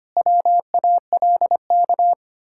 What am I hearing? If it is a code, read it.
Morse code, heard as a keyed tone, about 25 wpm, 700 Hz. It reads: WALK
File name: practice / 25wpm / Difficult Words - Speed-Racing with Hints - 25wpm